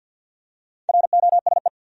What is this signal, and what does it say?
Morse code, heard as a keyed tone, about 51 wpm, 700 Hz. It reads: ROSE
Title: USA Common Names - Max 5 Letters - 50wpm